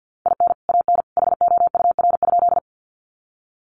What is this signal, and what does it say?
Morse code, heard as a keyed tone, about 50 wpm, 700 Hz. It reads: IN AN HOUR?
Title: Sentences from Top 400 Words 50wpm